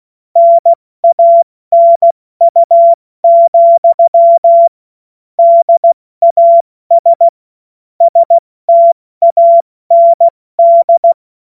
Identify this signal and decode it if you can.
Morse code, heard as a keyed tone, about 16 wpm, spaced as for 14 wpm, 680 Hz. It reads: NANU, DAS STAND
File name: Morsekurs / Witze / Witz 30. Dezember 2025 70 F80